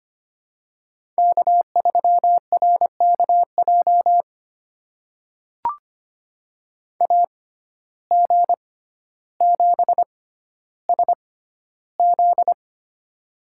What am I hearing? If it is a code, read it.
Morse code, heard as a keyed tone, about 25 wpm, spaced as for 13 wpm, 700 Hz. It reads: K3RKJ AG7SZ
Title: USA Call Signs - 5 Characters - 6x Character Spacing - 25wpm